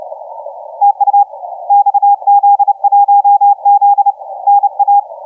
Morse code, heard as a keyed tone, about 31 wpm, 810 Hz, with radio noise. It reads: TU XZ1Z NA
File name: XZ1Z 1425UTC 15m